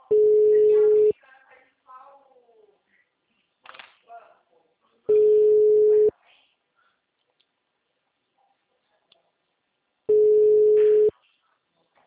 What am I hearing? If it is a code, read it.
Ringback tone (Europe, 425 Hz), with speech.